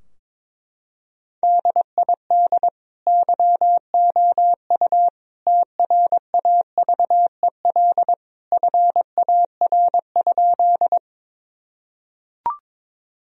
Morse code, heard as a keyed tone, about 22 wpm, 700 Hz. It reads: DID YOU TRAVEL FAR?
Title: Sentences from Top 1000 Words - Review Part 2 - 22wpm